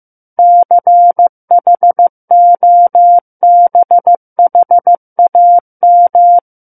Morse code, heard as a keyed tone, about 15 wpm, 700 Hz. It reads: CHOBHAM
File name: Book One / 06-Chapter 6 - 15 WPM